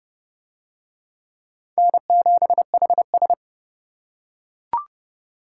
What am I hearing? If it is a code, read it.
Morse code, heard as a keyed tone, about 28 wpm, 700 Hz. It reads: N7HS